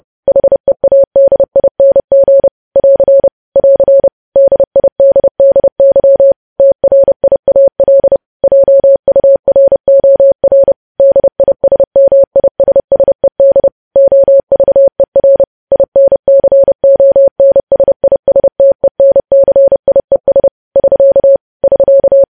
Morse code, heard as a keyed tone, about 30 wpm, 550 Hz. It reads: HEADING <AR> <AR> DIDDY TRIAL JUROR DISMISSED OVER INCONSISTENCIES <SK> <SK>
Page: news headlines in morse code at 30 wpm